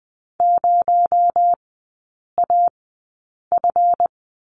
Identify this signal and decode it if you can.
Morse code, heard as a keyed tone, about 20 wpm, spaced as for 8 wpm, 700 Hz. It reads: 0AF
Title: 218 Lesson 35e - Number 0 - Three Characters 20wpm